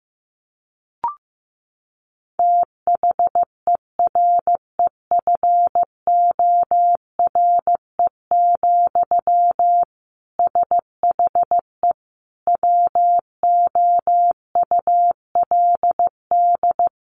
Morse code, heard as a keyed tone, about 15 wpm, 700 Hz. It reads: THEREFORE, SHE WOULD
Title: Sentences from Top 800 Words 15wpm